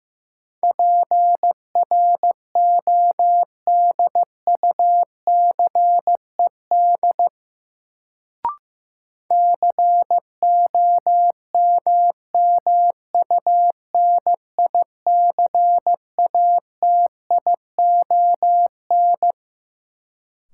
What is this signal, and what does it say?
Morse code, heard as a keyed tone, about 15 wpm, 700 Hz. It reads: PRODUCED COMMUNICATION
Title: Sentences from Top 1000 Words - Review Part 6 - New Word Drill - 15wpm